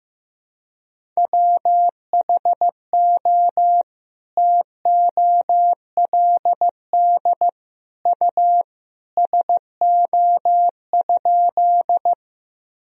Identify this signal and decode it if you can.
Morse code, heard as a keyed tone, about 15 wpm, 700 Hz. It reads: WHO TOLD U SO?
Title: Sentences from Top 400 Words and Abbreviations 15wpm